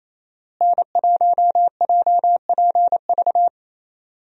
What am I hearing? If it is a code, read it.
Morse code, heard as a keyed tone, about 28 wpm, 700 Hz. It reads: N1JPV
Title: USA Call Signs - 5 Characters - Repeated 3x - 28wpm